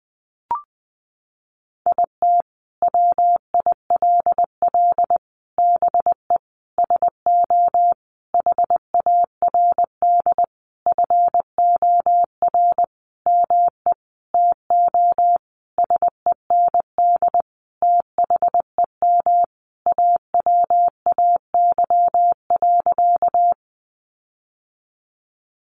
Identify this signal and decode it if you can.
Morse code, heard as a keyed tone, about 20 wpm, 700 Hz. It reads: IT WILL BE SO HARD FOR ME TO SEND THEM AWAY.
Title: Sentences from Top 1500 Words 20wpm